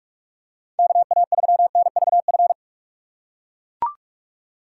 Morse code, heard as a keyed tone, about 42 wpm, 700 Hz. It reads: KA3NVF
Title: USA Call Signs - 6 Characters - 4x Character Spacing - 45wpm